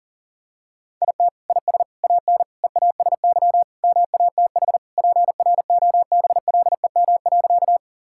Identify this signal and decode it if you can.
Morse code, heard as a keyed tone, about 40 wpm, 700 Hz. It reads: IT IS AN EASY MATH PROBLEM.
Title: Sentences from Top 1000 Words - Review Part 6 - 40wpm